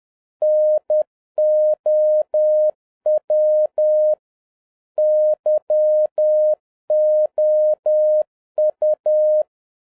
Morse code, heard as a keyed tone, about 10 wpm, 610 Hz. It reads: NOW YOU
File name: 2011-07-mp3 / n41 EXCITING QRP CW OPERATION 10wpm